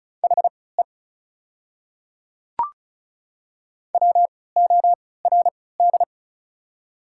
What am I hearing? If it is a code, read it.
Morse code, heard as a keyed tone, about 36 wpm, spaced as for 20 wpm, 700 Hz. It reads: HE WORD
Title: 3x Character Spacing - Top 100 Words - 35wpm